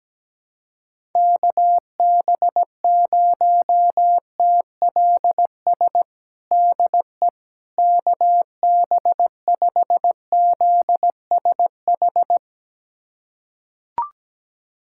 Morse code, heard as a keyed tone, about 17 wpm, 700 Hz. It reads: KB0TLS DE KB5ZSH
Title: USA Callsign DE Callsign - 6 Characters - 17wpm